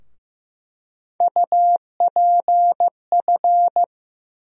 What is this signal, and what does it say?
Morse code, heard as a keyed tone, about 15 wpm, 700 Hz. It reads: UPF
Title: USA Call Sign Postfixes - 3 Letters 15wpm